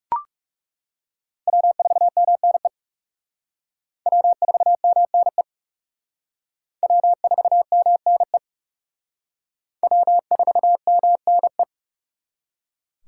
Morse code, heard as a keyed tone, about 36 wpm, 700 Hz. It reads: W4MNE W4MNE W4MNE W4MNE
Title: Speed Racing - USA 1x3 Call Signs 30wpm